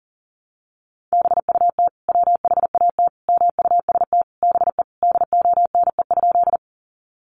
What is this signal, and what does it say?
Morse code, heard as a keyed tone, about 40 wpm, 700 Hz. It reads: BUT WHAT MUST BE DONE?
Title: Sentences from Top 500 Words 40wpm